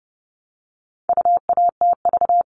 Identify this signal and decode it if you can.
Morse code, heard as a keyed tone, about 29 wpm, 700 Hz. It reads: UATV
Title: All Cut Numbers - 4-Digit 30wpm